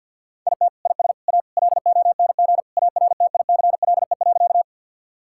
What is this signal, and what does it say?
Morse code, heard as a keyed tone, about 52 wpm, 700 Hz. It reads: IT IS A LONG ARTICLE.
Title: Sentences from Top 1000 Words - Review Part 5 - 50wpm